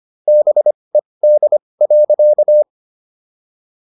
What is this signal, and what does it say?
Morse code, heard as a keyed tone, about 25 wpm, spaced as for 20 wpm, 600 Hz. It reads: BED.